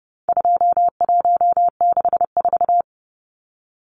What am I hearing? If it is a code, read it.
Morse code, heard as a keyed tone, about 30 wpm, 700 Hz. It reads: 2164
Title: Numbers - 4-Digit 30wpm